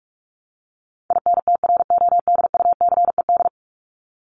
Morse code, heard as a keyed tone, about 44 wpm, 700 Hz. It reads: INTRODUCED
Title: ED Suffix - Top 100 Words with Base Words 45wpm